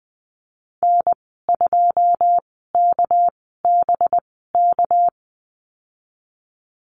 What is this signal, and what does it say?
Morse code, heard as a keyed tone, about 20 wpm, spaced as for 14 wpm, 700 Hz. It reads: N2KBK
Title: 245 Lesson 39g - Letter z - Call Signs 20wpm